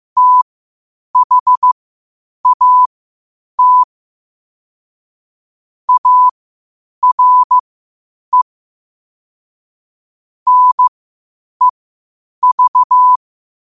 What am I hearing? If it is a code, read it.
Morse code, heard as a keyed tone, about 15 wpm, spaced as for 8 wpm, 1000 Hz. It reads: THAT ARE NEV
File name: qotdfiles / 8wpm